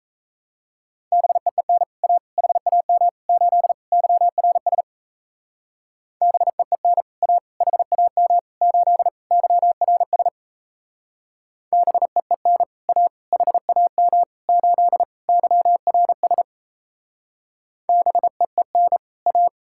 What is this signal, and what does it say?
Morse code, heard as a keyed tone, about 35 wpm, 700 Hz. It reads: BEEN A HAM 8 YRS BEEN A HAM 8 YRS BEEN A HAM 8 YRS BEEN A